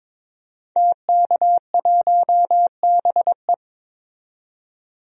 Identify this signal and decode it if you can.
Morse code, heard as a keyed tone, about 22 wpm, 700 Hz. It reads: TK1BE
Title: DXCC-100 Call Signs - 5 Characters - 22wpm